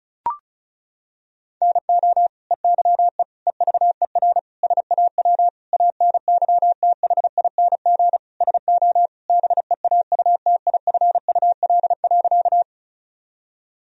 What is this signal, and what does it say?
Morse code, heard as a keyed tone, about 36 wpm, 700 Hz. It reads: NO EYE EVER SAW ANYTHING SO BEAUTIFUL.
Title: Sentences from Top 1000 Words - Review Part 1 - 35wpm